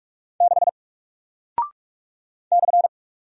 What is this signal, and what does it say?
Morse code, heard as a keyed tone, about 44 wpm, 700 Hz. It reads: 6 /
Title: Single Character (Letter:Number:Punctuation:Prosign) Alt 45wpm